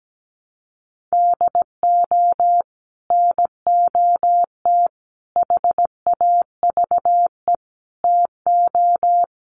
Morse code, heard as a keyed tone, about 17 wpm, 700 Hz. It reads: DO NOT HAVE TO
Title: Sets of 5 Words 17wpm